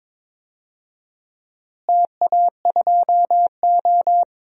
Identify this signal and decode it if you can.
Morse code, heard as a keyed tone, about 22 wpm, 700 Hz. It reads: TA2O
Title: DXCC-100 Call Signs - 4 Characters - Repeated 3x - 22wpm